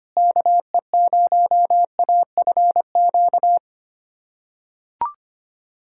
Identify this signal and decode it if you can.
Morse code, heard as a keyed tone, about 25 wpm, 700 Hz. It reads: KE0AFQ
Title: USA Callsign DE Callsign - 6 Characters - 25wpm